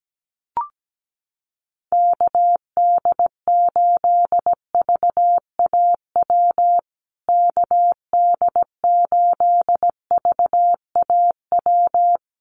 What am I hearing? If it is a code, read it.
Morse code, heard as a keyed tone, about 17 wpm, 700 Hz. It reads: KD8VAW KD8VAW
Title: USA Call Signs Repeated 3x 17wpm